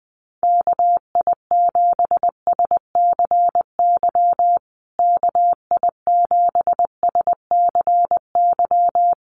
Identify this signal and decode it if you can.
Morse code, heard as a keyed tone, about 20 wpm, 700 Hz. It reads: KI7SCY KI7SCY
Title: USA Call Signs Repeated 3x 20wpm